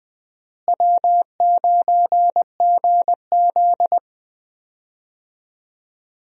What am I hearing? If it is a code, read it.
Morse code, heard as a keyed tone, about 20 wpm, 700 Hz. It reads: W9GZ
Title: USA Call Signs - 4 Characters - Repeated 3x - 20wpm